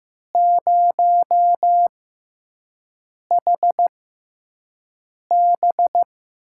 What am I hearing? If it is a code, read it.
Morse code, heard as a keyed tone, about 15 wpm, spaced as for 5 wpm, 700 Hz. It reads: 0HB
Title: USA Call Signs - 4 Characters - 6x Character Spacing - 15wpm